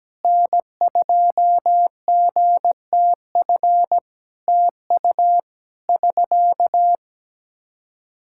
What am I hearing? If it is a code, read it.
Morse code, heard as a keyed tone, about 17 wpm, 700 Hz. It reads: N2GTF TU <SK>
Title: Practice QSO - Exchange 4 - Ending - 17wpm